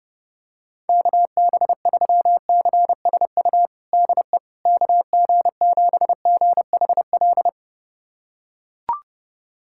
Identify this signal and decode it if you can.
Morse code, heard as a keyed tone, about 29 wpm, 700 Hz. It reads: KB3CSU DE KG7GHL
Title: USA Callsign DE Callsign - 6 Characters - 30wpm